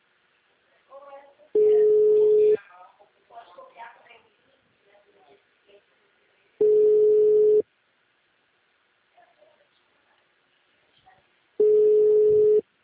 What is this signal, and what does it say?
Ringback tone (Europe, 425 Hz), with speech.